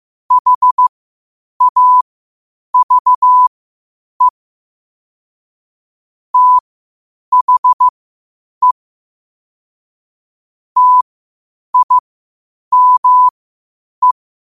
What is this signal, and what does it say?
Morse code, heard as a keyed tone, about 15 wpm, spaced as for 8 wpm, 1000 Hz. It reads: HAVE THE TIME
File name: qotdfiles / 8wpm